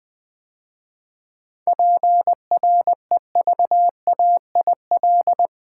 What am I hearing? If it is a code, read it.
Morse code, heard as a keyed tone, about 20 wpm, 700 Hz. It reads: PREVAIL
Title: 7 Letter Words - First 3 Letters Given - 20wpm